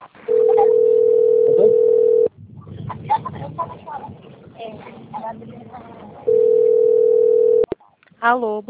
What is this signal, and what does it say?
Ringback tone (North America), 440 + 480 Hz, with speech.
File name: rg-1600-unknown- - - .11947